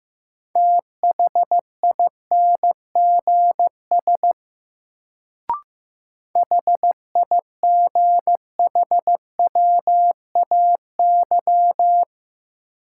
Morse code, heard as a keyed tone, about 15 wpm, 700 Hz. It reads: THINGS HIGHWAY
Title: Sentences from Top 1400 Words - New Word Drill Alt - 15wpm